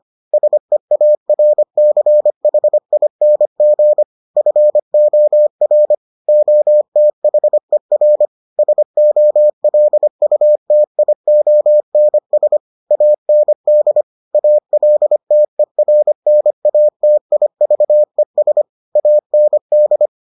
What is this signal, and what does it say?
Morse code, heard as a keyed tone, about 25 wpm, 600 Hz. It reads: SEARCHING FOR OTHER SOLUTIONS AND ALTERNATIVES AND